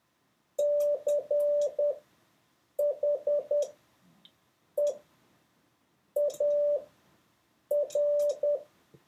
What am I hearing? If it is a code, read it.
Morse code, heard as a keyed tone, about 10 wpm, spaced as for 6 wpm, 580 Hz. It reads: CHEAR